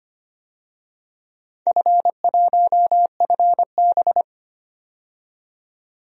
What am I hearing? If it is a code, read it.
Morse code, heard as a keyed tone, about 25 wpm, 700 Hz. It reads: F1FB
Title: DXCC-100 Call Signs - 4 Characters - 25wpm